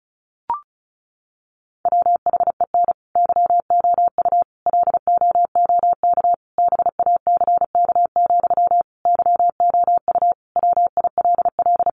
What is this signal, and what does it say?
Morse code, heard as a keyed tone, about 34 wpm, 700 Hz. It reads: WHEN YOU LOOK BACK, YOU WILL
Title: Sentences from Top 800 Words 35wpm